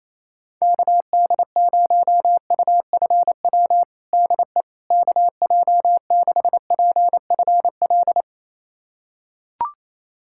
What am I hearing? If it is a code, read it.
Morse code, heard as a keyed tone, about 28 wpm, 700 Hz. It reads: KD0UFW DE KJ6PFL
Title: USA Callsign DE Callsign - 6 Characters - 28wpm